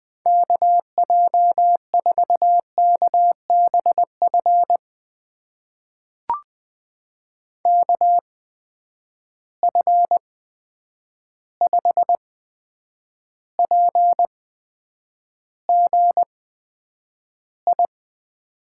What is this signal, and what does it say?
Morse code, heard as a keyed tone, about 20 wpm, spaced as for 9 wpm, 700 Hz. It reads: KJ4KBF KF5PGI
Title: USA Call Signs - 6 Characters - 8x Character Spacing - 20wpm